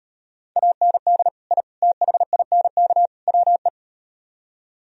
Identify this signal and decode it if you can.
Morse code, heard as a keyed tone, about 38 wpm, 700 Hz. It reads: AND I THINK WE